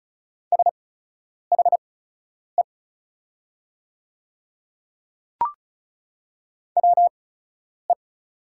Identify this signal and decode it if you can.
Morse code, heard as a keyed tone, about 36 wpm, spaced as for 10 wpm, 700 Hz. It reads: SHE WE
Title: 8x Character Spacing - Top 200 Words - 35wpm